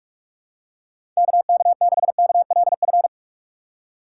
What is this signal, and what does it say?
Morse code, heard as a keyed tone, about 47 wpm, 700 Hz. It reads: KK6KLF
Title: USA Call Signs - 6 Characters - 3x Character Spacing - 45wpm